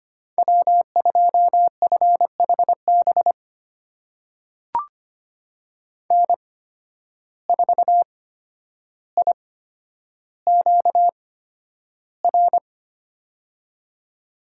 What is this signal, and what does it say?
Morse code, heard as a keyed tone, about 25 wpm, spaced as for 11 wpm, 700 Hz. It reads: W2FHB N4IQR
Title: USA Call Signs - 5 Characters - 8x Character Spacing - 25wpm